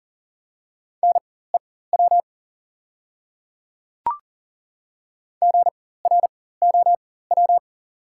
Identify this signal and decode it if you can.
Morse code, heard as a keyed tone, about 40 wpm, spaced as for 19 wpm, 690 Hz. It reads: NEW GROW
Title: 4x Character Spacing - Top 300 Words - 40wpm